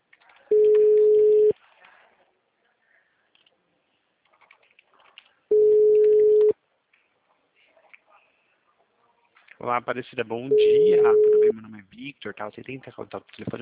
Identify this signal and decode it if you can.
Ringback tone (Europe, 425 Hz), with speech.
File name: out- -2011- - - .148